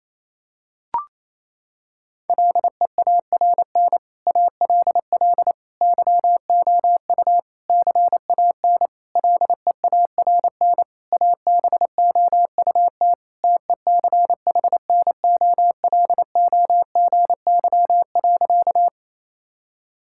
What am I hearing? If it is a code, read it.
Morse code, heard as a keyed tone, about 28 wpm, 700 Hz. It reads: LEARN ALL YOU CAN LEARN ABOUT TECHNOLOGY.
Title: Sentences from Top 1000 Words - Review Part 5 - 28wpm